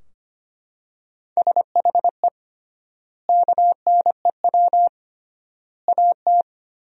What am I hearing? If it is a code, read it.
Morse code, heard as a keyed tone, about 25 wpm, 700 Hz. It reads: SHE KNEW AT
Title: 3x Word-Spacing - Sentences from Top 400 Words - 25wpm